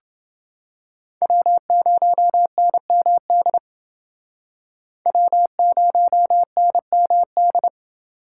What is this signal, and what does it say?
Morse code, heard as a keyed tone, about 29 wpm, 700 Hz. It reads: W0NMD W0NMD